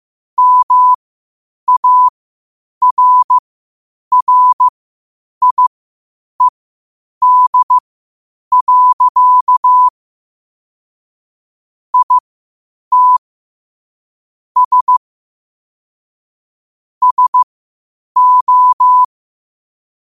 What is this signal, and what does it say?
Morse code, heard as a keyed tone, about 15 wpm, spaced as for 8 wpm, 1000 Hz. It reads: MARRIED. IT S SO
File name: qotdfiles / 8wpm